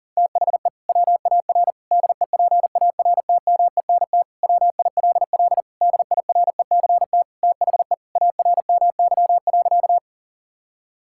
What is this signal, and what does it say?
Morse code, heard as a keyed tone, about 37 wpm, 700 Hz. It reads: THE WAR DEPARTMENT WILL DIRECT THE ARMY.
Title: Sentences from Top 1000 Words - Review Part 5 - 40wpm